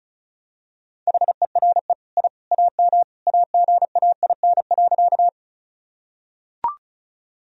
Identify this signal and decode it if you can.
Morse code, heard as a keyed tone, about 34 wpm, 700 Hz. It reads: HERE I AM AGAIN.
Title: Sentences from Top 1000 Words - Review Part 2 - 35wpm